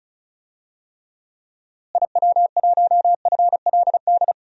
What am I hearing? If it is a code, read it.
Morse code, heard as a keyed tone, about 36 wpm, 700 Hz. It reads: IW1FLD
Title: DXCC-100 Call Signs - 6 Characters - Repeated 3x - 35wpm